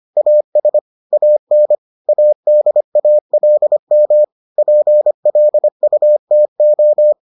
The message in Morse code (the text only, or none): AS AN ADALM PLUTO